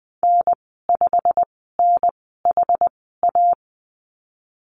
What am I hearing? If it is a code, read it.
Morse code, heard as a keyed tone, about 20 wpm, spaced as for 15 wpm, 700 Hz. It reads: N5NHA
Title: 128 Lesson 21g - Number 6 - Call Signs 20wpm